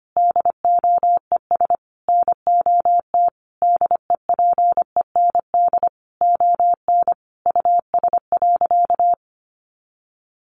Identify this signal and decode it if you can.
Morse code, heard as a keyed tone, about 25 wpm, 700 Hz. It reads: DOES NOT DEPEND ON US.
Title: Sentences from Top 1300 Words 25wpm